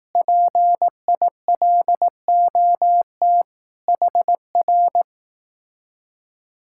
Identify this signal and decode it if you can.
Morse code, heard as a keyed tone, about 18 wpm, 700 Hz. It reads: PILOT HR
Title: Speed-Racing Common QSO Phrases Encore - 15wpm